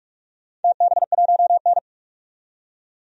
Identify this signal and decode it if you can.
Morse code, heard as a keyed tone, about 45 wpm, 700 Hz. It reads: TB1N
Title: DXCC-100 Call Signs - 4 Characters - 45wpm